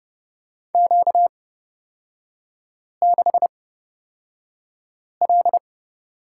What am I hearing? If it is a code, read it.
Morse code, heard as a keyed tone, about 31 wpm, spaced as for 5 wpm, 700 Hz. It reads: Q6L
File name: Lesson 38e - Letter q - Three Characters 30wpm